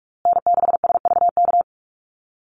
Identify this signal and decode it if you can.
Morse code, heard as a keyed tone, about 44 wpm, 700 Hz. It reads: N6SVK
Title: USA Call Signs Repeated 2x 45wpm